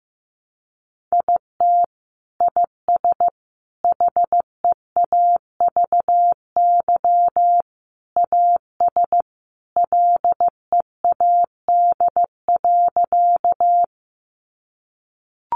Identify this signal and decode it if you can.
Morse code, heard as a keyed tone, about 15 wpm, 700 Hz. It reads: IT IS HEAVY AS LEAD.
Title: Sentences from Top 1100 Words 15wpm